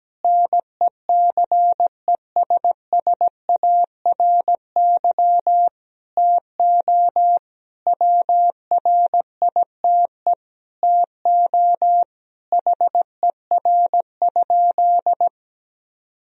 Morse code, heard as a keyed tone, about 17 wpm, 700 Hz. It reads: NECESSARY TO WRITE TO HER?
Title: Sentences from Top 1000 Words - Review Part 1 - 17wpm